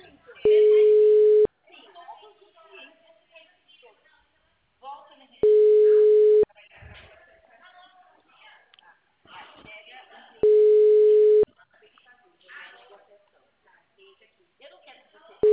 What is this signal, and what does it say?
Ringback tone (Europe, 425 Hz), with speech.
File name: out- -2007- - - .2812